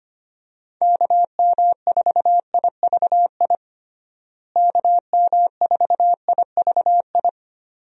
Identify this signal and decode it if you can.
Morse code, heard as a keyed tone, about 25 wpm, 700 Hz. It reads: KM4IVI KM4IVI
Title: USA Call Signs - 6 Characters - Repeated 2x - 25wpm